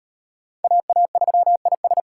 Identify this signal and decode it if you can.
Morse code, heard as a keyed tone, about 37 wpm, 700 Hz. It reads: AA3IS